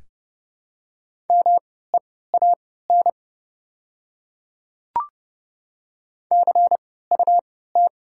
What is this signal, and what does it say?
Morse code, heard as a keyed tone, about 30 wpm, spaced as for 17 wpm, 700 Hz. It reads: MEAN CUT
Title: 3x Character Spacing - Top 300 Words - 30wpm